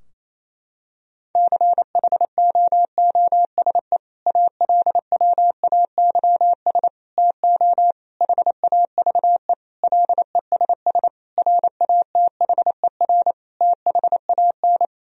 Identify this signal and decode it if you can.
Morse code, heard as a keyed tone, about 28 wpm, 700 Hz. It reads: CHOOSE ALWAYS TO HAVE LESS RATHER THAN
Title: Sentences from Top 1000 Words - Review Part 4 - 28wpm